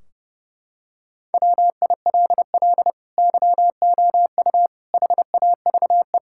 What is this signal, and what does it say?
Morse code, heard as a keyed tone, about 32 wpm, 700 Hz. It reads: WILL YOU HAVE
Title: Sentences from Top 1600 Words 30wpm